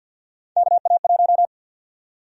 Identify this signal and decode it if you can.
Morse code, heard as a keyed tone, about 50 wpm, 700 Hz. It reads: KA1
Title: USA Call Sign Prefixes - Repeated 2x - 50wpm